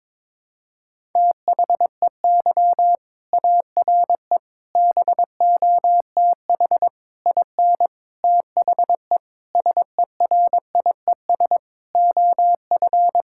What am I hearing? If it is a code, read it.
Morse code, heard as a keyed tone, about 22 wpm, 700 Hz. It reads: THEY ARE BOTH IN THE SERIES OF
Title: Sentences from Top 1000 Words - Review Part 5 - 22wpm